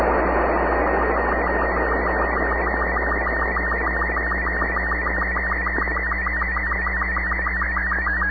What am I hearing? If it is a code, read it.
SSTV (slow-scan television), Robot 36.
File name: sat / AO-51 14Jul2010 SSTV